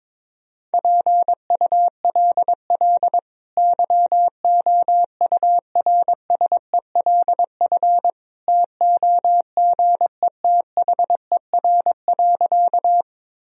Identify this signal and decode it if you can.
Morse code, heard as a keyed tone, about 22 wpm, 700 Hz. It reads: PULL YOURSELF TOGETHER.
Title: Sentences from Top 1000 Words - Review Part 2 - 22wpm